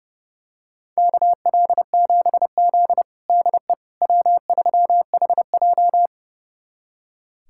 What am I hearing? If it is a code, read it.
Morse code, heard as a keyed tone, about 32 wpm, 700 Hz. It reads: KL7Z DE W3HJ
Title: USA Callsign DE Callsign - 4 Characters - 30wpm